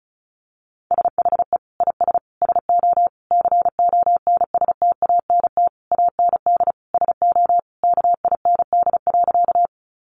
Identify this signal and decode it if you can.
Morse code, heard as a keyed tone, about 34 wpm, 700 Hz. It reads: SHE IS SO CONSTANT AND SO KIND.
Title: Sentences from Top 1500 Words 35wpm